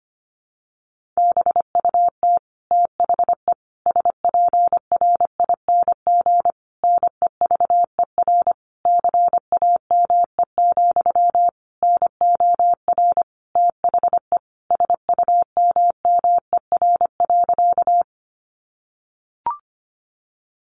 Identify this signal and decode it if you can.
Morse code, heard as a keyed tone, about 25 wpm, 700 Hz. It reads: BUT THE SPRING NEVER CAME, NOR THE SUMMER.
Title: Sentences from Top 1500 Words 25wpm